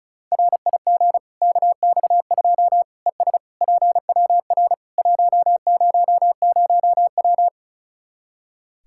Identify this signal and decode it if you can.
Morse code, heard as a keyed tone, about 34 wpm, 700 Hz. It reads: RIG KX2 ES PWR 100W
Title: Practice QSO - Exchange 2 - Info - 35wpm